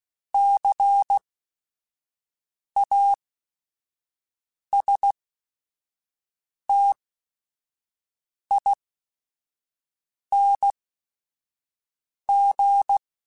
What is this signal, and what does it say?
Morse code, heard as a keyed tone, about 16 wpm, spaced as for 5 wpm, 800 Hz. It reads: CASTING